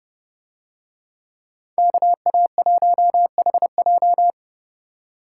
Morse code, heard as a keyed tone, about 31 wpm, 700 Hz. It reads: KA1HJ